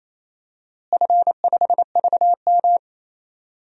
Morse code, heard as a keyed tone, about 28 wpm, 700 Hz. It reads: F5VM
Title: DXCC-100 Call Signs - 4 Characters - Repeated 2x - 28wpm